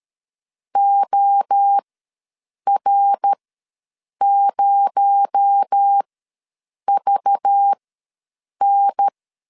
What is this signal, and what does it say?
Morse code, heard as a keyed tone, about 13 wpm, spaced as for 7 wpm, 800 Hz. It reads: OR0VN
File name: LEZIONI / 8 YJ45 / 8 Y ws 12wpm 3409